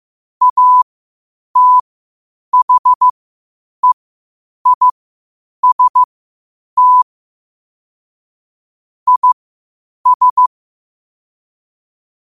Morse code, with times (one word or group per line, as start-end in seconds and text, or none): 0.4-7.0 ATHEIST
9.1-10.5 IS